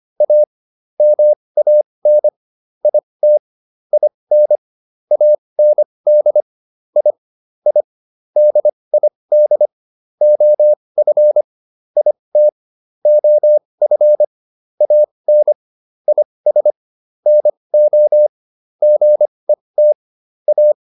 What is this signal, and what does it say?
Morse code, heard as a keyed tone, about 25 wpm, spaced as for 20 wpm, 600 Hz. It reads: A MAN IT IN AND I I DID OF IT OF AN IS NO GET A